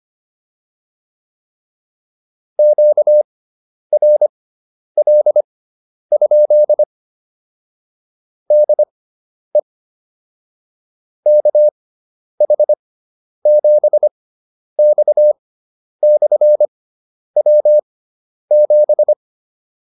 Morse code, heard as a keyed tone, about 25 wpm, spaced as for 10 wpm, 600 Hz. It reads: QRL? DE KH7X/W7
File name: QSO104 10